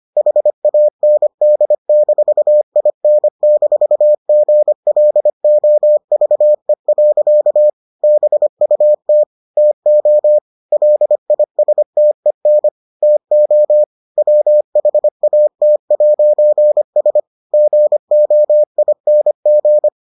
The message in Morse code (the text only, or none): HAND-IN-GLOVE. BUT TO LISTEN TO WHAT'S GOING